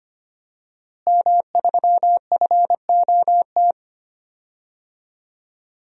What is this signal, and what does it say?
Morse code, heard as a keyed tone, about 25 wpm, 700 Hz. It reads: M3FOT
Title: DXCC-100 Call Signs - 5 Characters - Repeated 3x - 25wpm